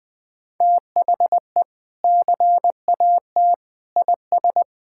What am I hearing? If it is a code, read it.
Morse code, heard as a keyed tone, about 20 wpm, 700 Hz. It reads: THE CAT IS
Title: Sentences with 2-Word Phrases - Spoken Beginning - 20wpm